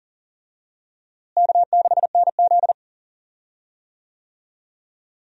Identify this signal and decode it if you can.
Morse code, heard as a keyed tone, about 40 wpm, 700 Hz. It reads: K6NZ
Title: USA Call Signs - 4 Characters - Repeated 3x - 40wpm